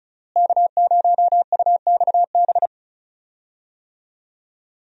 Morse code, heard as a keyed tone, about 36 wpm, 700 Hz. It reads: K0UXB
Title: USA Call Signs - 5 Characters - Repeated 2x - 35wpm